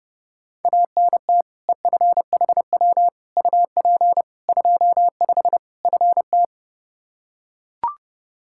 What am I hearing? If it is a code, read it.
Morse code, heard as a keyed tone, about 32 wpm, 700 Hz. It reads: ANT EFHW UP 25 FT
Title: Practice QSO - Exchange 2 - Info - 30wpm